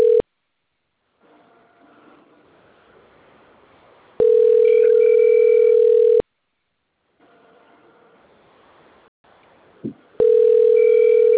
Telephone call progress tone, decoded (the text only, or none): ringback tone